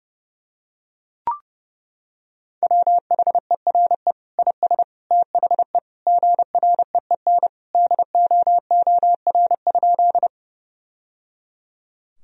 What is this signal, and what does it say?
Morse code, heard as a keyed tone, about 30 wpm, 700 Hz. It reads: WHERE IS THE GREEN DOOR?
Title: Sentences from Top 1000 Words - Review Part 2 - 30wpm